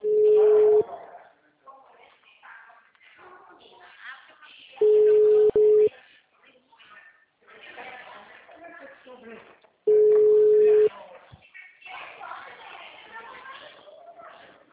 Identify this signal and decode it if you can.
Ringback tone (Europe, 425 Hz), with speech.
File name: out- -2011- - - .1100